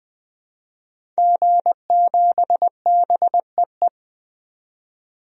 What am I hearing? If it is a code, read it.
Morse code, heard as a keyed tone, about 20 wpm, 700 Hz. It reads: G7BEE
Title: DXCC-100 Call Signs - 5 Characters - 20wpm